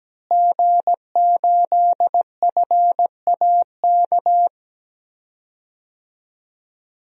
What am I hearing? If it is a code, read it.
Morse code, heard as a keyed tone, about 17 wpm, 700 Hz. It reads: G8FAK
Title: DXCC-100 Call Signs - 5 Characters - Repeated 2x - 17wpm